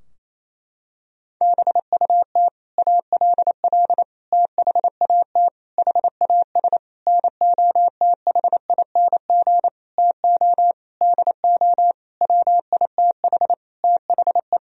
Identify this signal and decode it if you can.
Morse code, heard as a keyed tone, about 27 wpm, 700 Hz. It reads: BUT ALL THAT HAS NOTHING TO DO WITH THE
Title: Sentences from Top 1000 Words - Review Part 3 - 28wpm